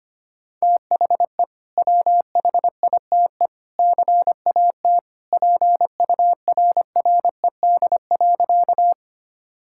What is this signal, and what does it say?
Morse code, heard as a keyed tone, about 25 wpm, 700 Hz. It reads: THE WHITE CAT PURRED.
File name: practice / 25wpm / Sentences with 2-Word Phrases - 25wpm